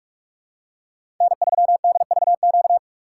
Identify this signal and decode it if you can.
Morse code, heard as a keyed tone, about 42 wpm, 700 Hz. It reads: N3DVQ
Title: USA Call Signs - 5 Characters - 6x Character Spacing - 45wpm